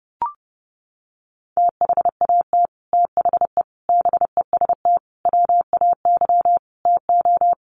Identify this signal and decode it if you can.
Morse code, heard as a keyed tone, about 30 wpm, 700 Hz. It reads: THAT THE BEST WAY TO
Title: Sets of 5 Words Encore2 30wpm